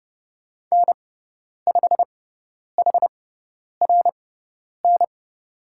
Morse code, heard as a keyed tone, about 30 wpm, spaced as for 10 wpm, 700 Hz. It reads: N5HRN
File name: Lesson 16g - Letter c - Call Signs 30wpm